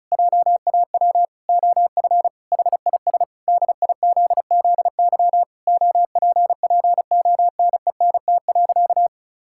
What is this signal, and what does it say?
Morse code, heard as a keyed tone, about 34 wpm, 700 Hz. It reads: JAW OF HIS DIZZY OPPONENT.
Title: Pangram Challenge v2 - 35wpm